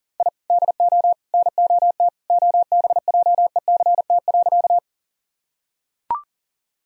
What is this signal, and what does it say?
Morse code, heard as a keyed tone, about 40 wpm, 700 Hz. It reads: I DO NOT OBJECT.
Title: Sentences from Top 1000 Words - Review Part 3 - 40wpm